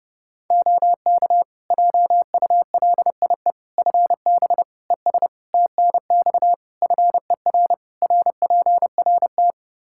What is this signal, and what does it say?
Morse code, heard as a keyed tone, about 30 wpm, 700 Hz. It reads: OK JULIE FB ES TNX FER RPRT
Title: Practice QSO - Exchange 2 - Info - 30wpm